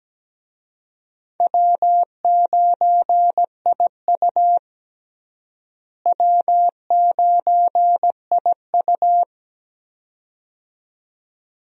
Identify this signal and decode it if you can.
Morse code, heard as a keyed tone, about 17 wpm, 700 Hz. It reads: W9IU W9IU
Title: USA Call Signs - 4 Characters - Repeated 2x - 17wpm